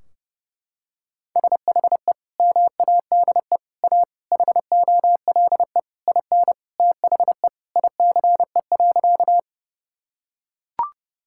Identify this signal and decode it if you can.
Morse code, heard as a keyed tone, about 30 wpm, 700 Hz. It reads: SHE MADE A HOLE IN THE ICE.
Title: Sentences from Top 1600 Words 30wpm